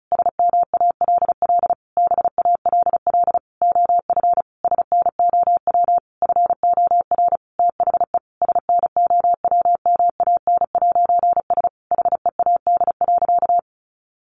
Morse code, heard as a keyed tone, about 34 wpm, 700 Hz. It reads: SMALL BALL OF SNOW FOR THE SNOWMAN'S HEAD.
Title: Easy Phrases 35wpm